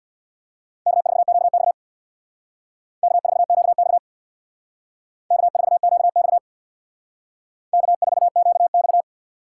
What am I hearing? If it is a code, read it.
Morse code, heard as a keyed tone, about 62 wpm, 700 Hz. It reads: K4QX K4QX K4QX K4QX
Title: Speed Racing - USA 1x2 Call Signs 50wpm